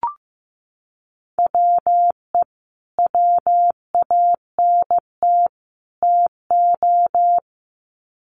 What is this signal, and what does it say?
Morse code, heard as a keyed tone, about 15 wpm, 700 Hz. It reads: WE WANT TO
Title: Sets of 3 Words 15wpm